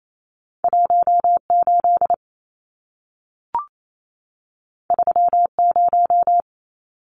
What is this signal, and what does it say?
Morse code, heard as a keyed tone, about 28 wpm, 700 Hz. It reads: 18 30
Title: Numbers - 2-Digit 28wpm